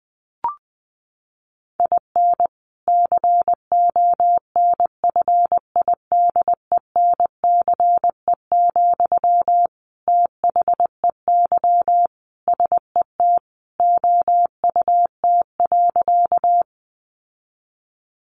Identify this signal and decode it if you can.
Morse code, heard as a keyed tone, about 20 wpm, 700 Hz. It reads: IN CONFIDENCE, THEY SET OUT.
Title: Sentences from Top 1500 Words 20wpm